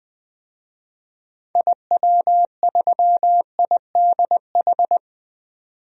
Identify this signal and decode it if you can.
Morse code, heard as a keyed tone, about 20 wpm, 700 Hz. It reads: IW3IDH
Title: DXCC-100 Call Signs - 6 Characters - 20wpm